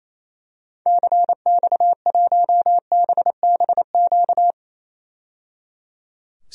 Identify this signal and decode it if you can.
Morse code, heard as a keyed tone, about 27 wpm, 700 Hz. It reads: CX1BBQ
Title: DXCC-100 Call Signs - 6 Characters - 28wpm